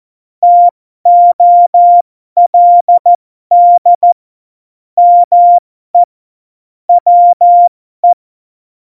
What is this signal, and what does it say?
Morse code, heard as a keyed tone, about 14 wpm, spaced as for 12 wpm, 700 Hz. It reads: TOLD ME WE
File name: Morsekurs / Witze